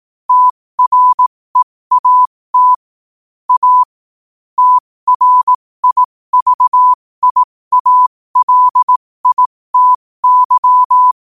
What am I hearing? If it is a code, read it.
Morse code, heard as a keyed tone, about 18 wpm, spaced as for 15 wpm, 1000 Hz. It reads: TREAT A TRIVIALITY
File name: qotdfiles / 15wpm